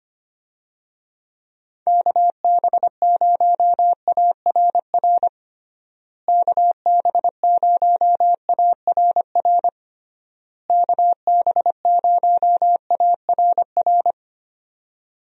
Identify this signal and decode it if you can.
Morse code, heard as a keyed tone, about 25 wpm, 700 Hz. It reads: KB0ARR KB0ARR KB0ARR
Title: USA Call Signs - 6 Characters - Repeated 3x - 25wpm